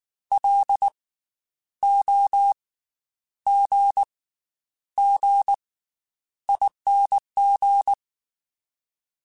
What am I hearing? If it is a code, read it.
Morse code, heard as a keyed tone, about 19 wpm, spaced as for 9 wpm, 800 Hz. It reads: LOGGING
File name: Special INGs / Special INGs-mp3s / Slow ing @18